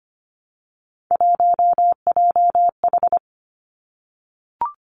Morse code, heard as a keyed tone, about 25 wpm, 700 Hz. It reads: 1JH
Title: Three Challenging Letter-Number Combinations 25wpm